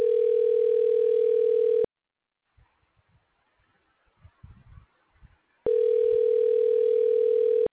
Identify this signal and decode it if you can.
Ringback tone (North America), 440 + 480 Hz.